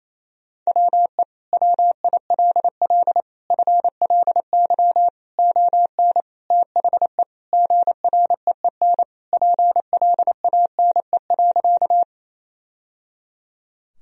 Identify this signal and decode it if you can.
Morse code, heard as a keyed tone, about 28 wpm, 700 Hz. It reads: WE WILL FLY ON THE GREEN PLANE.
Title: Sentences from Top 1000 Words - Review Part 3 - 28wpm